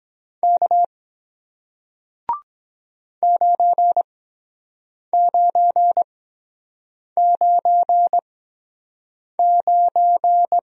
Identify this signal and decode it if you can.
Morse code, heard as a keyed tone, about 23 wpm, spaced as for 7 wpm, 700 Hz. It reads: K 9999